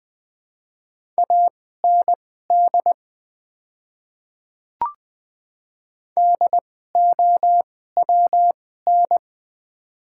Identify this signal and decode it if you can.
Morse code, heard as a keyed tone, about 21 wpm, spaced as for 15 wpm, 700 Hz. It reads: AND DOWN
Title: 2x Character Spacing - Top 100 Words - 20wpm